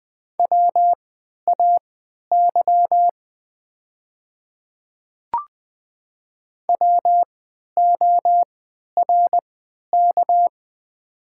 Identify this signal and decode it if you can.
Morse code, heard as a keyed tone, about 20 wpm, spaced as for 11 wpm, 700 Hz. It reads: WAY WORK
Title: 3x Character Spacing - Top 200 Words - 20wpm